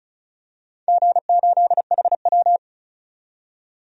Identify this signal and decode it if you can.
Morse code, heard as a keyed tone, about 34 wpm, 700 Hz. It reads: G8HW
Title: DXCC-100 Call Signs - 4 Characters - 35wpm